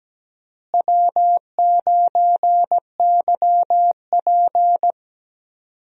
Morse code, heard as a keyed tone, about 17 wpm, 700 Hz. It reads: W9YP
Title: USA Call Signs - 4 Characters - Repeated 3x - 17wpm